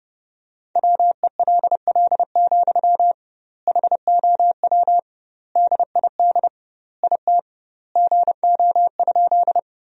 Morse code, heard as a keyed tone, about 30 wpm, 700 Hz. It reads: WELL, HOW DID IT GO?
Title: 2x Word-Spacing - Sentences from Top 200 Words - 30wpm